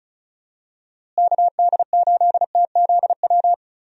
Morse code, heard as a keyed tone, about 34 wpm, 700 Hz. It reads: KD8TZW
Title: USA Call Signs - 6 Characters - Repeated 3x - 35wpm